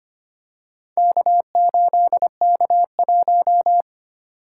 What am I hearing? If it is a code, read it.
Morse code, heard as a keyed tone, about 25 wpm, 700 Hz. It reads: K8K1
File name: practice / 25wpm / Four Letter-Number Combinations - 25wpm